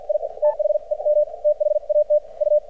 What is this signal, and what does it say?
Morse code, heard as a keyed tone, about 50 wpm, spaced as for 42 wpm, 600 Hz. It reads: I THEM THAT U
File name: archived / CRSnet